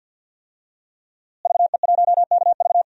Morse code, heard as a keyed tone, about 50 wpm, 700 Hz. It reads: VE1KV
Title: DXCC-100 Call Signs - 5 Characters - Repeated 2x - 50wpm